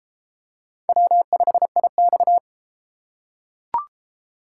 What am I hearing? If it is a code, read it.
Morse code, heard as a keyed tone, about 30 wpm, 700 Hz. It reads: W5IX